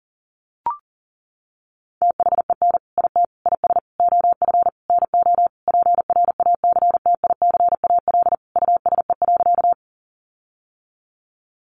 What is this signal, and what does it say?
Morse code, heard as a keyed tone, about 40 wpm, 700 Hz. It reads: THEN IT IS OF NO PRACTICAL USE.
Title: Sentences from Top 1400 Words 40wpm